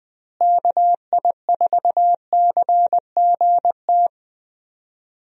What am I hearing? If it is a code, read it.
Morse code, heard as a keyed tone, about 20 wpm, 700 Hz. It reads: KI4CGT
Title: USA Call Signs - 6 Characters - Repeated 2x - 20wpm